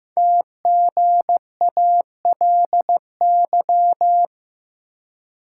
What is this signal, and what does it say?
Morse code, heard as a keyed tone, about 15 wpm, 700 Hz. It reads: TGALY
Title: Code Groups - 5 Letters - 15wpm